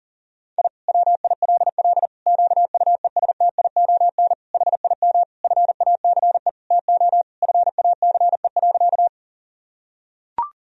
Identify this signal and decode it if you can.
Morse code, heard as a keyed tone, about 40 wpm, 700 Hz. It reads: I WILL QUESTION HIM FACE TO FACE.
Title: Sentences from Top 1000 Words - Review Part 2 - 40wpm